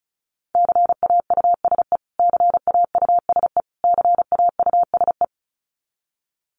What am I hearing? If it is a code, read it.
Morse code, heard as a keyed tone, about 34 wpm, 700 Hz. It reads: CAUSE CAUSE CAUSE
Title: Top 300 Words Repeated 35wpm